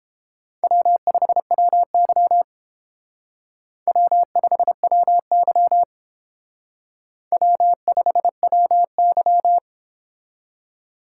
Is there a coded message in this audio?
Yes: Morse code from 0.6 to 9.6 s, heard as a keyed tone, about 31 wpm, 700 Hz. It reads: W5WY W5WY W5WY